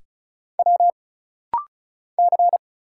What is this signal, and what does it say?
Morse code, heard as a keyed tone, about 34 wpm, 700 Hz. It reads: W C
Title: Single Letters - Rapid-Fire 35wpm